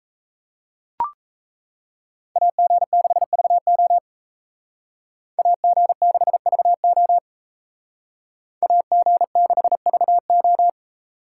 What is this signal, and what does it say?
Morse code, heard as a keyed tone, about 34 wpm, 700 Hz. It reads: AG6VO AG6VO AG6VO
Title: Speed Racing - USA 2x2 Call Signs 28wpm